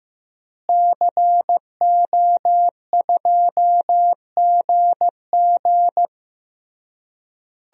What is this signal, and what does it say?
Morse code, heard as a keyed tone, about 15 wpm, 700 Hz. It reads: CO2GG
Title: DXCC-100 Call Signs - 5 Characters - Repeated 3x - 15wpm